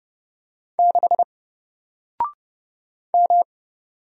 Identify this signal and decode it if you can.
Morse code, heard as a keyed tone, about 31 wpm, 700 Hz. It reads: 6 M
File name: Single-Letter-Number-30wpm